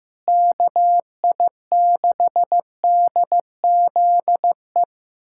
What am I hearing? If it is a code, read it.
Morse code, heard as a keyed tone, about 16 wpm, 700 Hz. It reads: KI6DZE
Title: USA Callsign DE Callsign 15wpm